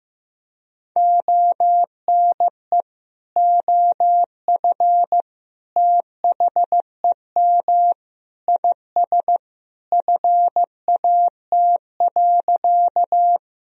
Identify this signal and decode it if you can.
Morse code, heard as a keyed tone, about 15 wpm, 700 Hz. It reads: ONE OF THEM IS FAT.
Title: Sentences from Top 1000 Words - Review Part 4 - 15wpm